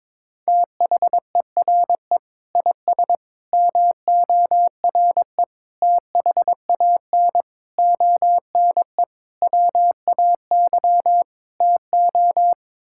Morse code, heard as a keyed tone, about 22 wpm, 700 Hz. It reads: THERE IS MORE THAN ONE WAY TO
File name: practice / 22wpm / Sets of 7 Words - 22wpm